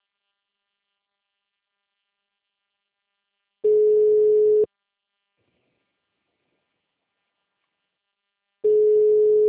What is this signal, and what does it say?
Ringback tone (Europe, 425 Hz).
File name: out- -2028- - - .11